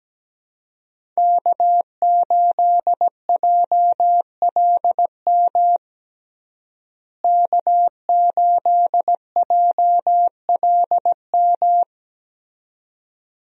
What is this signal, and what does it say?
Morse code, heard as a keyed tone, about 17 wpm, 700 Hz. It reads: K8JLM K8JLM
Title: USA Call Signs - 5 Characters - Repeated 2x - 17wpm